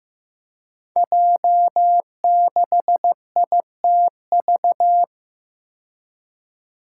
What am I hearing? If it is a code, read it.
Morse code, heard as a keyed tone, about 15 wpm, 700 Hz. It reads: J6ITV
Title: Five Challenging Letter-Number Combinations 15wpm